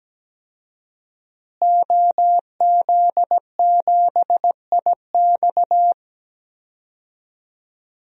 Morse code, heard as a keyed tone, about 17 wpm, 700 Hz. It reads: OZ7IX
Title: DXCC-100 Call Signs - 5 Characters - Repeated 3x - 17wpm